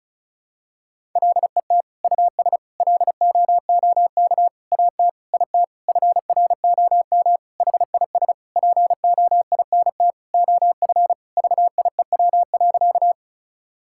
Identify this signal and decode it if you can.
Morse code, heard as a keyed tone, about 34 wpm, 700 Hz. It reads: LET US LOOK AT IT FROM HIS POINT OF VIEW.
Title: Sentences from Top 1000 Words - Review Part 6 - 35wpm